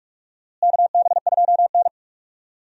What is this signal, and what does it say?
Morse code, heard as a keyed tone, about 48 wpm, 700 Hz. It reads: KB2N
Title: USA Call Signs - 4 Characters - Repeated 3x - 45wpm